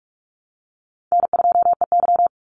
Morse code, heard as a keyed tone, about 44 wpm, 700 Hz. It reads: N2EY
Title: Call Signs Repeated 3x - 45wpm